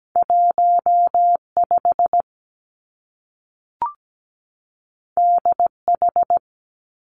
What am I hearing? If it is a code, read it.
Morse code, heard as a keyed tone, about 17 wpm, 700 Hz. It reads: 1 5 D H
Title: Two Challenging Letter-Number Combinations 17wpm